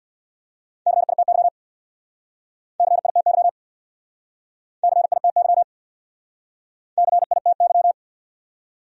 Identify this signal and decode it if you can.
Morse code, heard as a keyed tone, about 64 wpm, 700 Hz. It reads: CITY CITY CITY CITY